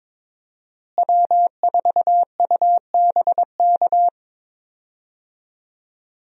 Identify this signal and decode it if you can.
Morse code, heard as a keyed tone, about 22 wpm, 700 Hz. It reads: W4UBK
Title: USA Call Signs - 5 Characters - Repeated 3x - 22wpm